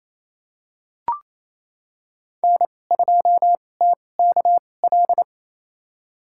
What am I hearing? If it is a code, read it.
Morse code, heard as a keyed tone, about 28 wpm, spaced as for 20 wpm, 700 Hz. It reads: N2TKL
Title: USA Call Signs - 5 Characters - 2x Character Spacing - 28wpm